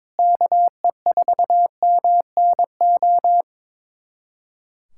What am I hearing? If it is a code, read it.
Morse code, heard as a keyed tone, about 21 wpm, 700 Hz. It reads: KE4MNO
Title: USA Callsign DE Callsign 22wpm